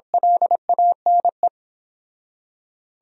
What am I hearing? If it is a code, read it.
Morse code, heard as a keyed tone, about 26 wpm, 700 Hz. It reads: LANE
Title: Speed-Racing Top 500 Words 22wpm